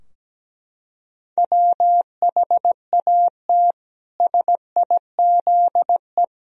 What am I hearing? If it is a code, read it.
Morse code, heard as a keyed tone, about 17 wpm, 700 Hz. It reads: WHAT SIZE
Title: Sentences from Top 1000 Words - Review Part 2 - 17wpm